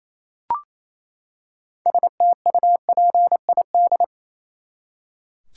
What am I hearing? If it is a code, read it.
Morse code, heard as a keyed tone, about 28 wpm, 700 Hz. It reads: STUPID
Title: Six Letter Words - 28wpm